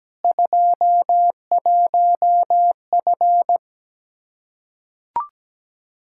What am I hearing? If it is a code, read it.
Morse code, heard as a keyed tone, about 17 wpm, 700 Hz. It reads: 21F
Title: Practice QSO - Exchange 2 - Info - 17wpm